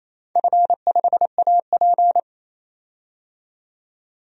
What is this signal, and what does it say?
Morse code, heard as a keyed tone, about 27 wpm, 690 Hz. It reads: F5AP
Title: DXCC-100 Call Signs - 4 Characters - 28wpm